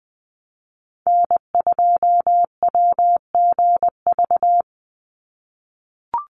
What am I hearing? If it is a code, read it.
Morse code, heard as a keyed tone, about 19 wpm, 700 Hz. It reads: N2WGV
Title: USA Call Signs 20wpm